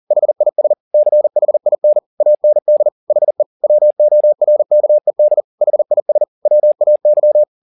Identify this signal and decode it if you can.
Morse code, heard as a keyed tone, about 40 wpm, 600 Hz. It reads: HIS CHIN AND HE WORKED HIS WAY